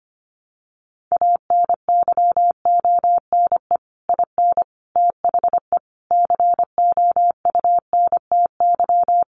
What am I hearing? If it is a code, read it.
Morse code, heard as a keyed tone, about 25 wpm, 700 Hz. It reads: ANYONE IN THE COUNTY
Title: Sentences from Top 1200 Words 25wpm